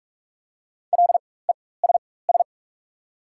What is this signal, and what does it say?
Morse code, heard as a keyed tone, about 48 wpm, spaced as for 21 wpm, 690 Hz. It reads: LESS
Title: 4x Character Spacing - Top 500 Words - 45wpm